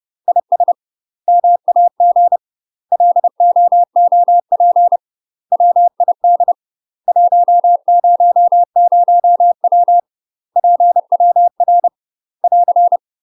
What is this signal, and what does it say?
Morse code, heard as a keyed tone, about 30 wpm, 700 Hz. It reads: IS MAG LOOP WID 100W PWR <AR>
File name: UK MASTER QSOs TRAINING / 30WPM / UK MASTER QSOs 0015